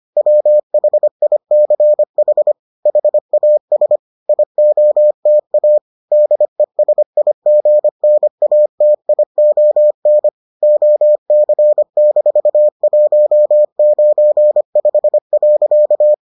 Morse code, heard as a keyed tone, about 25 wpm, 600 Hz. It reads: WHICH HAS IOTA DESIGNATION OC-195.